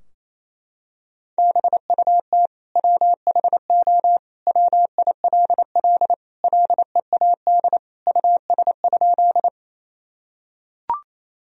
Morse code, heard as a keyed tone, about 28 wpm, 700 Hz. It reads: BUT WHO WILL LEAD US?
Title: Sentences from Top 1000 Words - Review Part 2 - 28wpm